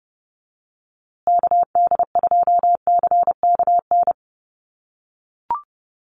Morse code, heard as a keyed tone, about 28 wpm, 700 Hz. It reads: KD2CKN
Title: USA Call Signs Repeated 2x 30wpm